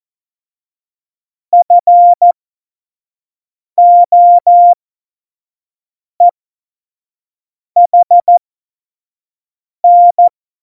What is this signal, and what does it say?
Morse code, heard as a keyed tone, about 14 wpm, spaced as for 5 wpm, 700 Hz. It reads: FOEHN
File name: Morsekurs / Witze / Witz 24. Oktober 17, Tempo 25